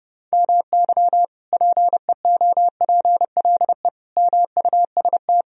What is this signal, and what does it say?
Morse code, heard as a keyed tone, about 30 wpm, 700 Hz. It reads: MY PEOPLE MUST
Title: Sentences from Top 1000 Words - Review Part 3 - 30wpm